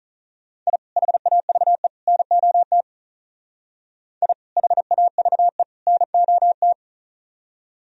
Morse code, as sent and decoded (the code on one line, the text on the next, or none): .. / .... .- ...- . / -. --- - / .. / .... .- ...- . / -. --- -
I HAVE NOT I HAVE NOT